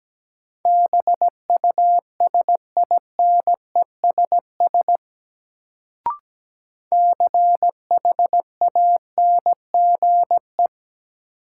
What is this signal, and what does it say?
Morse code, heard as a keyed tone, about 17 wpm, 700 Hz. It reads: BUSINESS CHANGE
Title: Sentences from Top 600 Words - New Word Drill - 17wpm